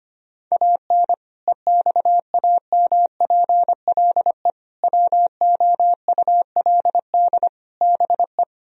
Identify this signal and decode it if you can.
Morse code, heard as a keyed tone, about 25 wpm, 700 Hz. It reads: AN EXAMPLE WOULD BE
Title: Sentences from Top 1000 Words - Review Part 1 - 25wpm